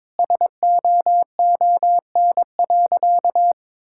Morse code, heard as a keyed tone, about 21 wpm, 700 Hz. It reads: SOON.
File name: practice / 22wpm / Sentences with 2-Word Phrases - 22wpm